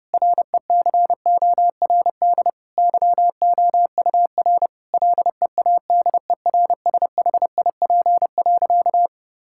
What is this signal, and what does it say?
Morse code, heard as a keyed tone, about 30 wpm, 700 Hz. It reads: RECORD YOUR LEADERSHIP.
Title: Sentences from Top 1600 Words 30wpm